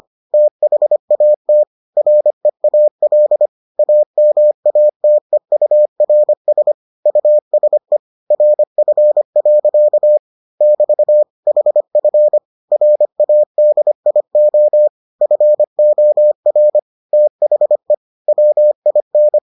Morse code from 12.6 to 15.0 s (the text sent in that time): RADIO